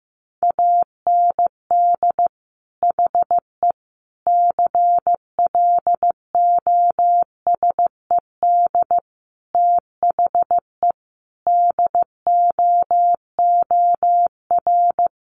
Morse code, heard as a keyed tone, about 15 wpm, 700 Hz. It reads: AND HE CLOSED THE DOOR
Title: Sentences from Top 1500 Words 15wpm